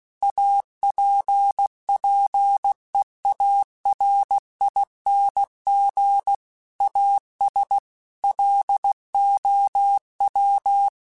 Morse code, heard as a keyed tone, about 16 wpm, 800 Hz. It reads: APPEARING AS LOW